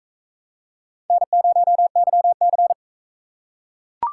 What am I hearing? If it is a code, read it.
Morse code, heard as a keyed tone, about 44 wpm, 700 Hz. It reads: N0YC